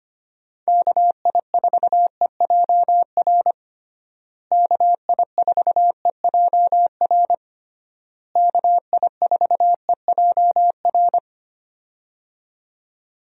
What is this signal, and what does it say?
Morse code, heard as a keyed tone, about 25 wpm, 700 Hz. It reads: KI4EJR KI4EJR KI4EJR
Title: USA Call Signs - 6 Characters - Repeated 3x - 25wpm